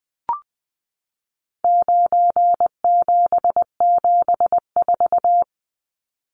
Morse code, heard as a keyed tone, about 20 wpm, 700 Hz. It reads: 9774
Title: Numbers - 4-Digit 20wpm